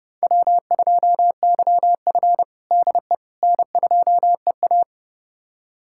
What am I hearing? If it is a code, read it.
Morse code, heard as a keyed tone, about 32 wpm, 700 Hz. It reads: W2YF DE N2EA
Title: USA Callsign DE Callsign - 4 Characters - 30wpm